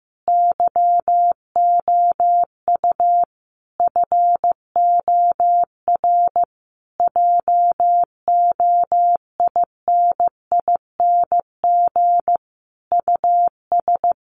Morse code, heard as a keyed tone, about 15 wpm, 700 Hz. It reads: YOU FOR JOINING US
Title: Sets of 5 Words 15wpm